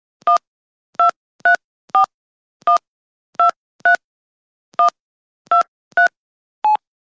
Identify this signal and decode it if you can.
DTMF, heard as telephone touch tones, tones of about 120 ms. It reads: 1234123123
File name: Phone Dialing